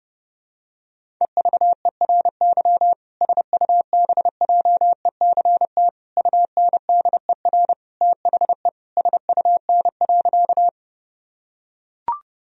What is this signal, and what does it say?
Morse code, heard as a keyed tone, about 30 wpm, 700 Hz. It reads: EVERY SUBJECT UNDER THE SUN.
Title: Sentences from Top 1000 Words - Review Part 1 - 30wpm